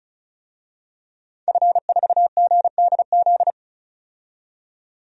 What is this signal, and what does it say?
Morse code, heard as a keyed tone, about 34 wpm, 700 Hz. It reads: F4GDZ
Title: DXCC-100 Call Signs - 5 Characters - 35wpm